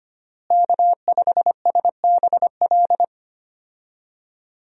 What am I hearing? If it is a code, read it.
Morse code, heard as a keyed tone, about 25 wpm, 700 Hz. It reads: K5SBL
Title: USA Call Signs - 5 Characters - Repeated 2x - 25wpm